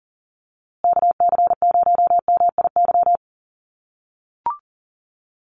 Morse code, heard as a keyed tone, about 40 wpm, 700 Hz. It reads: KC0MIY
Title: USA Call Signs Repeated 3x 40wpm